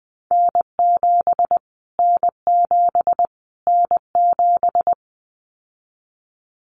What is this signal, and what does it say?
Morse code, heard as a keyed tone, about 20 wpm, 700 Hz. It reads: N7 N7 N7
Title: Callsign Prefixes for Top 25 DXCC - Repeated 3x - 20wpm